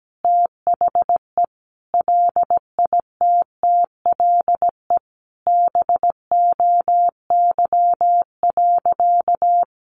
Morse code, heard as a keyed tone, about 17 wpm, 700 Hz. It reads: THE LITTLE BOY.
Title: Sentences from Top 1000 Words 17wpm